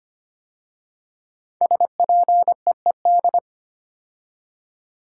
Morse code, heard as a keyed tone, about 25 wpm, 700 Hz. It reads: SPEED
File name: 5 Letter Words - First 3 Letters Given - 25wpm